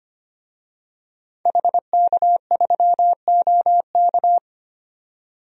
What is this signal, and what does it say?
Morse code, heard as a keyed tone, about 25 wpm, 700 Hz. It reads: HK3OK
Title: DXCC-100 Call Signs - 5 Characters - 25wpm